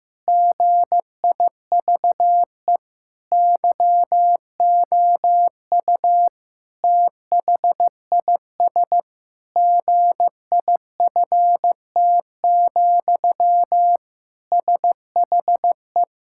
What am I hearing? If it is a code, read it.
Morse code, heard as a keyed tone, about 15 wpm, 700 Hz. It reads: GIVE YOU THIS GIFT, SHE
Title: Sentences from Top 1000 Words - Review Part 6 - 15wpm